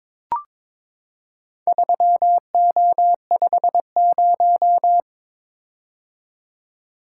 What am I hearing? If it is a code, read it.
Morse code, heard as a keyed tone, about 22 wpm, 700 Hz. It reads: 3O50
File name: practice / 22wpm / Four Letter-Number Combinations - 22wpm